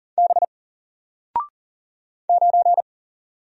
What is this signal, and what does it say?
Morse code, heard as a keyed tone, about 42 wpm, 700 Hz. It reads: B 9
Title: Single Letter-Number 40wpm